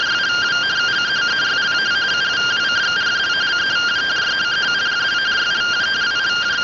Two-tone teleprinter signal, about 170 Hz shift.